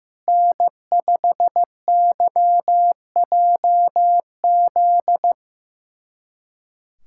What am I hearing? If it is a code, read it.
Morse code, heard as a keyed tone, about 16 wpm, 700 Hz. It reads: N5YJZ
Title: USA Callsign DE Callsign - 5 Characters - 15wpm